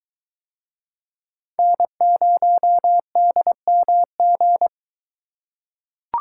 Morse code, heard as a keyed tone, about 21 wpm, 700 Hz. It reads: N0DMG